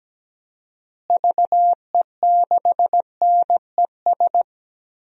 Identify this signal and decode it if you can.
Morse code, heard as a keyed tone, about 17 wpm, 700 Hz. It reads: VE6NES
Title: DXCC-100 Call Signs - 6 Characters - Repeated 3x - 17wpm